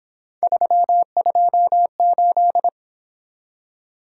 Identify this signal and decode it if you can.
Morse code, heard as a keyed tone, about 27 wpm, 700 Hz. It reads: 328
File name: Speed-Racing - Three Letter-Number Combinations - 22wpm